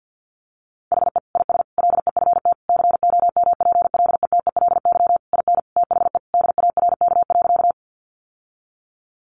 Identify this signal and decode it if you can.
Morse code, heard as a keyed tone, about 50 wpm, 700 Hz. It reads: HE IS LEFT COMPLETELY IN THE DARK.
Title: Sentences from Top 1100 Words 50wpm